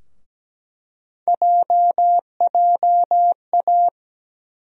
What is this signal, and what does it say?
Morse code, heard as a keyed tone, about 17 wpm, 700 Hz. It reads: JJA
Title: USA Call Sign Postfixes - 3 Letters 17wpm